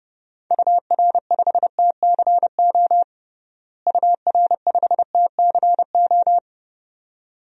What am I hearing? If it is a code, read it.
Morse code, heard as a keyed tone, about 30 wpm, 700 Hz. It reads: UR5TCO UR5TCO
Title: DXCC-100 Call Signs - 6 Characters - Repeated 2x - 30wpm